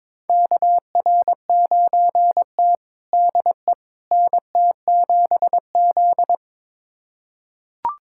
Morse code, heard as a keyed tone, about 22 wpm, 700 Hz. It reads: KR9T DE NT7Z
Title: USA Callsign DE Callsign - 4 Characters - 22wpm